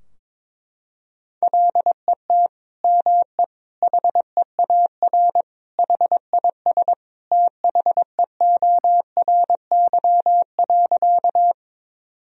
Morse code, heard as a keyed tone, about 22 wpm, 700 Hz. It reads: LET ME HEAR HIS THEORY.
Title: Sentences from Top 1000 Words - Review Part 3 - 22wpm